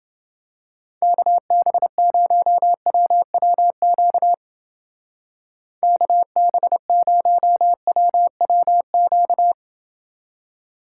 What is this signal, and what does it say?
Morse code, heard as a keyed tone, about 29 wpm, 700 Hz. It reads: KB0WWQ KB0WWQ